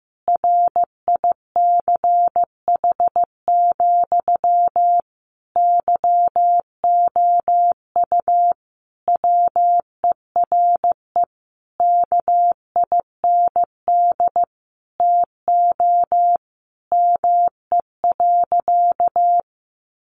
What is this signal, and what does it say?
Morse code, heard as a keyed tone, about 14 wpm, 700 Hz. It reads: RICH, YOU WERE KIND TO ME.
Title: Sentences from Top 1100 Words 15wpm